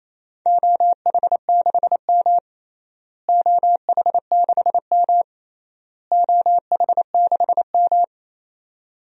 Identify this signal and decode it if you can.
Morse code, heard as a keyed tone, about 28 wpm, 700 Hz. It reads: OH6M OH6M OH6M
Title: DXCC-100 Call Signs - 4 Characters - Repeated 3x - 28wpm